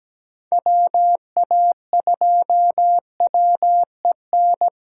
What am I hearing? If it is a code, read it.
Morse code, heard as a keyed tone, about 17 wpm, 700 Hz. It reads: WA2WEN
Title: USA Callsign DE Callsign 17wpm